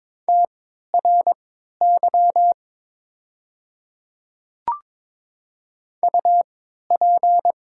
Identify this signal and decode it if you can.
Morse code, heard as a keyed tone, about 22 wpm, spaced as for 13 wpm, 700 Hz. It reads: TRY UP
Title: 3x Character Spacing - Top 300 Words - 22wpm